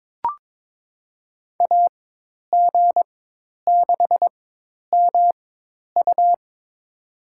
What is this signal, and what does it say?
Morse code, heard as a keyed tone, about 22 wpm, spaced as for 10 wpm, 700 Hz. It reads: AG6MU
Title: USA Call Signs - 5 Characters - 4x Character Spacing - 22wpm